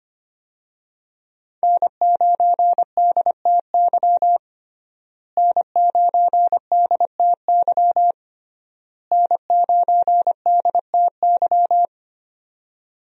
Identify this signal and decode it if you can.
Morse code, heard as a keyed tone, about 25 wpm, 700 Hz. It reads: N9DTY N9DTY N9DTY
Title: USA Call Signs - 5 Characters - Repeated 3x - 25wpm